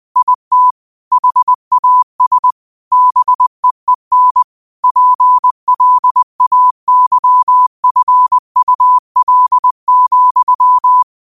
Morse code, heard as a keyed tone, about 20 wpm, 1000 Hz. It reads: IT HAS BEEN PLAYFUL,
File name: qotdfiles / 20wpm